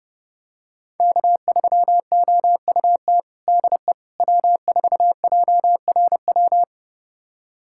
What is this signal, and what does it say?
Morse code, heard as a keyed tone, about 30 wpm, 700 Hz. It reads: K3OUT DE W4JRW
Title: USA Callsign DE Callsign - 5 Characters - 30wpm